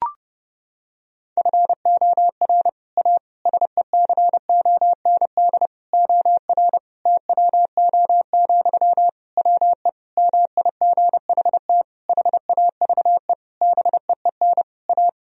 Morse code, heard as a keyed tone, about 30 wpm, 700 Hz. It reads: FOR A SECOND OR TWO, WE MIGHT HAVE BEEN A
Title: Sentences from Top 1000 Words - Review Part 1 - 30wpm